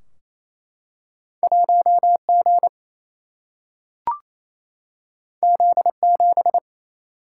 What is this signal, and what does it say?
Morse code, heard as a keyed tone, about 28 wpm, 700 Hz. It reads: 1 G Z 7
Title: Single-Letter Single-Number 28wpm